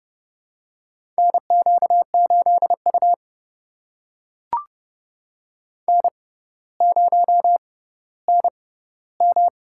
Morse code, heard as a keyed tone, about 30 wpm, spaced as for 15 wpm, 700 Hz. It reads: NQ8U N0NM
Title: USA Call Signs - 6x Character Spacing - 30wpm